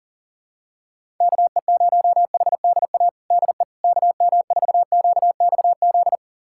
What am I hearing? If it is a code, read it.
Morse code, heard as a keyed tone, about 40 wpm, 700 Hz. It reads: KE0HDA DE KM4QXZ
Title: USA Callsign DE Callsign - 6 Characters - 40wpm